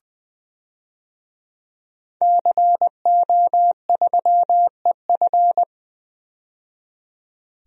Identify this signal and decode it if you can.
Morse code, heard as a keyed tone, about 20 wpm, 700 Hz. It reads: CO3EF
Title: DXCC-100 Call Signs - 5 Characters - 20wpm